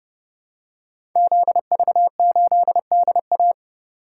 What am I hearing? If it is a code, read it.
Morse code, heard as a keyed tone, about 30 wpm, 700 Hz. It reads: ZV8DA
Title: Code Groups - 5 Letter:Numbers - 30wpm